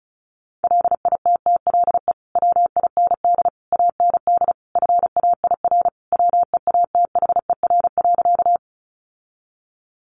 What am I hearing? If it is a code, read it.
Morse code, heard as a keyed tone, about 34 wpm, 700 Hz. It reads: LITTLE WIND AND FAIR WEATHER.
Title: Sentences from Top 1300 Words 35wpm